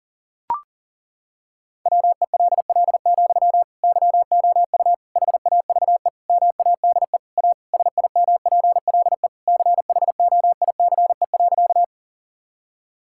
Morse code, heard as a keyed tone, about 37 wpm, 700 Hz. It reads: WELL, YOU HAVE MADE A SIMPLE CHOICE.
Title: Sentences from Top 1000 Words - Review Part 6 - 40wpm